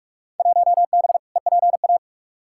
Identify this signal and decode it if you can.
Morse code, heard as a keyed tone, about 45 wpm, 700 Hz. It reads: 1B EPA
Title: Field Day Exchanges - 45wpm